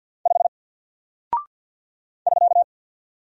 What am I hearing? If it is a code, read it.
Morse code, heard as a keyed tone, about 48 wpm, 700 Hz. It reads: 5 <SK>
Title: Single Character (Letter:Number:Punctuation:Prosign) Alt 50wpm